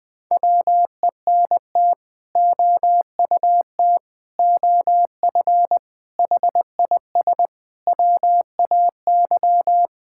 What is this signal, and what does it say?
Morse code, heard as a keyed tone, about 20 wpm, 700 Hz. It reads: WENT OUT OF HIS WAY
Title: Sets of 6 Words - 20wpm